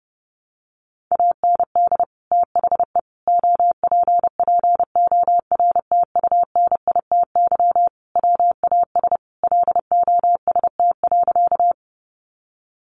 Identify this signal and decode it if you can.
Morse code, heard as a keyed tone, about 30 wpm, 700 Hz. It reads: AND THE OPPORTUNITY WAS LOST.
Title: Sentences from Top 1400 Words 30wpm